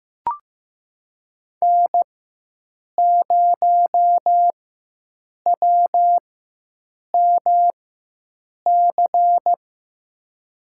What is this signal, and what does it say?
Morse code, heard as a keyed tone, about 15 wpm, spaced as for 7 wpm, 700 Hz. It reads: N0WMC
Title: USA Call Signs - 4x Character Spacing - 15wpm